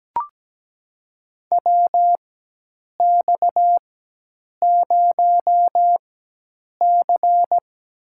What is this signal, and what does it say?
Morse code, heard as a keyed tone, about 17 wpm, spaced as for 8 wpm, 700 Hz. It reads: WX0C
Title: USA Call Signs - 4x Character Spacing - 17wpm